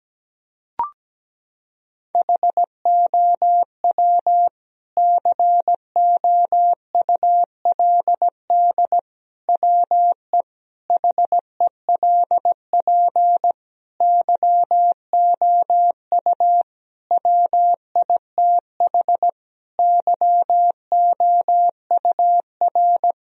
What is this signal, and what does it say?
Morse code, heard as a keyed tone, about 17 wpm, 700 Hz. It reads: HOW COULD WE HELP YOU WITH YOUR
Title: Sentences from Top 1000 Words - Review Part 3 - 17wpm